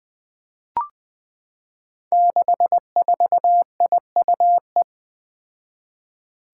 Morse code, heard as a keyed tone, about 20 wpm, 700 Hz. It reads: 64IUE
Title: Five Challenging Letter-Number Combinations 20wpm